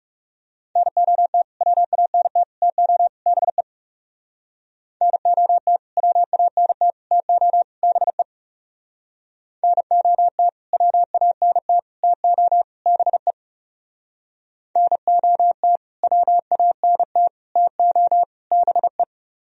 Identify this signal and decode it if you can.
Morse code, heard as a keyed tone, about 36 wpm, 700 Hz. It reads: NOT WANT TO BE NOT WANT TO BE NOT WANT TO BE NOT WANT TO BE
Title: Speed-Racing - Sets of 4 Words 30wpm